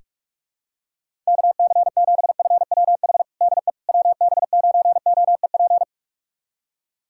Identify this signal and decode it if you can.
Morse code, heard as a keyed tone, about 46 wpm, 690 Hz. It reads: KC7FWH DE WB9OEP
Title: USA Callsign DE Callsign - 6 Characters - 45wpm